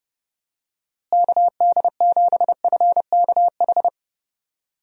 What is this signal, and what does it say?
Morse code, heard as a keyed tone, about 32 wpm, 700 Hz. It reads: KD7FKH
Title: USA Call Signs - 6 Characters - Repeated 2x - 30wpm